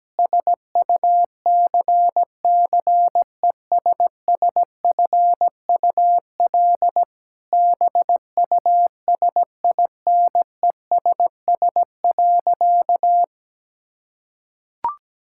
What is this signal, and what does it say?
Morse code, heard as a keyed tone, about 17 wpm, 700 Hz. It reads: SUCCESSFUL BUSINESS.
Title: Sentences from Top 1000 Words - Review Part 6 - 17wpm